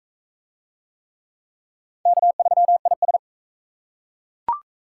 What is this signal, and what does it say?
Morse code, heard as a keyed tone, about 40 wpm, 700 Hz. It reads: K3IS